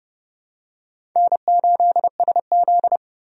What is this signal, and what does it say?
Morse code, heard as a keyed tone, about 31 wpm, 700 Hz. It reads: N8SZ
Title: USA Call Signs - 4 Characters - 30wpm